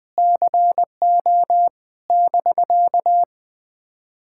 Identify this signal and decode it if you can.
Morse code, heard as a keyed tone, about 20 wpm, 700 Hz. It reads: CO <BK>
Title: POTA QSO Practice - 20wpm